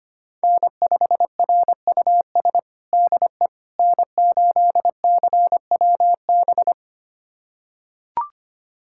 Morse code, heard as a keyed tone, about 25 wpm, 700 Hz. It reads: N5RUS DE N8CWB
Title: USA Callsign DE Callsign - 5 Characters - 25wpm